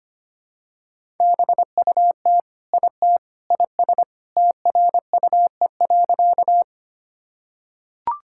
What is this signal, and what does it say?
Morse code, heard as a keyed tone, about 25 wpm, 700 Hz. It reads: BUT IT IS TRUE.
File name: practice / 25wpm / Sentences with 3-Word Phrases - 25wpm